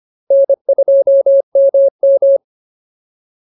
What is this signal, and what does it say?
Morse code, heard as a keyed tone, about 25 wpm, 550 Hz. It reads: N2MM
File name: POTA309 25